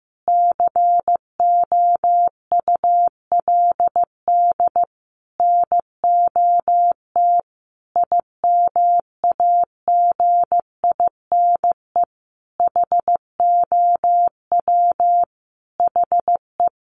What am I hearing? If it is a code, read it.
Morse code, heard as a keyed tone, about 15 wpm, 700 Hz. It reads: COULD NOT IMAGINE HOW HE
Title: Sentences from Top 1400 Words 15wpm